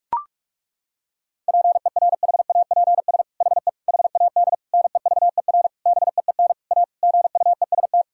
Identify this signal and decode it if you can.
Morse code, heard as a keyed tone, about 41 wpm, 700 Hz. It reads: PERHAPS HE HAD NEVER BEEN A GUEST
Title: Sentences from Top 1600 Words 45wpm